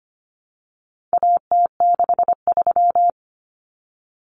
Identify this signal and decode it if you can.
Morse code, heard as a keyed tone, about 25 wpm, 700 Hz. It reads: AT63
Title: Common Cut Numbers - 4-Digit 25wpm